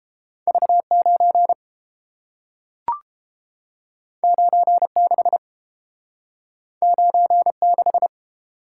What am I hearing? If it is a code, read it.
Morse code, heard as a keyed tone, about 30 wpm, 700 Hz. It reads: V9 96 96